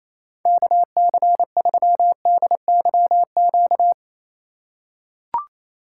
Morse code, heard as a keyed tone, about 27 wpm, 700 Hz. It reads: KC3DYQ
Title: USA Call Signs - 4x Character Spacing - 28wpm